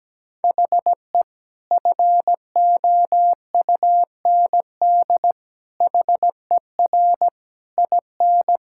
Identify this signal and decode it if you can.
Morse code, heard as a keyed tone, about 17 wpm, 700 Hz. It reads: HE FOUND HER IN
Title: Sentences from Top 1600 Words 17wpm